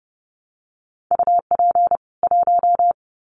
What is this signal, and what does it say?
Morse code, heard as a keyed tone, about 30 wpm, 700 Hz. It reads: UP 1
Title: Common QSO Phrases 30wpm